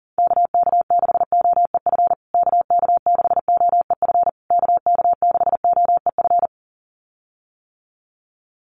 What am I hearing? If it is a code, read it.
Morse code, heard as a keyed tone, about 40 wpm, 700 Hz. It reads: KK6OEF KK6OEF KK6OEF
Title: USA Call Signs Repeated 3x 40wpm